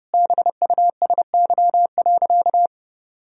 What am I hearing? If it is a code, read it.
Morse code, heard as a keyed tone, about 30 wpm, 700 Hz. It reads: BUSY.
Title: Sentences from Top 1000 Words - Review Part 3 - 30wpm